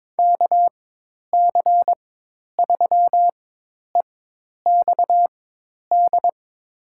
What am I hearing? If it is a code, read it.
Morse code, heard as a keyed tone, about 22 wpm, spaced as for 10 wpm, 700 Hz. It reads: KC3EXD
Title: USA Call Signs - 6 Characters - 4x Character Spacing - 22wpm